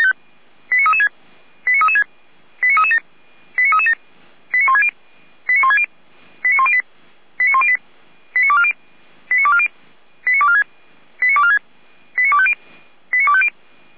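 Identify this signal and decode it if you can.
Five-tone selective call (ZVEI), heard as quick runs of five tones, tones of about 70 ms. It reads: 89336 89336 89337 89337 89160 89160 89108 89108 89240 89240 89246 89246 89250 89250
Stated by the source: ZVEI five-tone selective call received by SDR.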